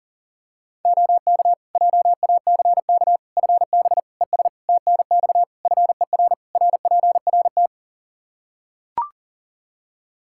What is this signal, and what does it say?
Morse code, heard as a keyed tone, about 40 wpm, 700 Hz. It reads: OK JACK FB ES TNX FER RPRT
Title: Practice QSO - Exchange 2 - Info - 40wpm